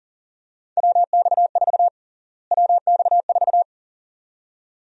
Morse code, heard as a keyed tone, about 40 wpm, 700 Hz. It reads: WX4 WX4
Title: USA Call Sign Prefixes - Repeated 2x - 40wpm